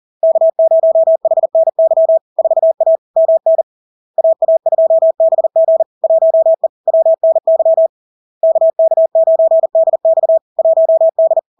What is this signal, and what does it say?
Morse code, heard as a keyed tone, about 40 wpm, 650 Hz. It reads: K0HNY 4A MN AA2BG 1E WNY KK9DX 1D
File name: LARK-FD-2022-Log-in-CW-40wpm-650-Hz